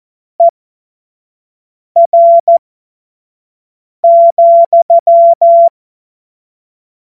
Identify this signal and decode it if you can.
Morse code, heard as a keyed tone, about 14 wpm, spaced as for 5 wpm, 680 Hz. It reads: ER,
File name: Morsekurs / Witze / Witz 09. Februar 2021, 25 F70